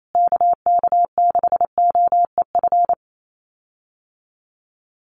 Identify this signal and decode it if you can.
Morse code, heard as a keyed tone, about 28 wpm, 700 Hz. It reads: KK6OEF
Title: USA Call Signs Repeated 3x 28wpm